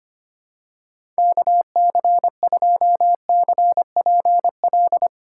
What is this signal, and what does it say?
Morse code, heard as a keyed tone, about 25 wpm, 700 Hz. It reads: KC2CPL
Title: USA Call Signs - 6 Characters - 2x Character Spacing - 25wpm